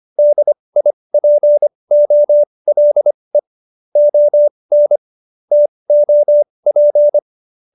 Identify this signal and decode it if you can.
Morse code, heard as a keyed tone, about 25 wpm, spaced as for 20 wpm, 590 Hz. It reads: DIPOLE ON TOP